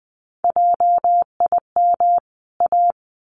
Morse code, heard as a keyed tone, about 19 wpm, 700 Hz. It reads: JIM A
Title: CWT Exchange 20wpm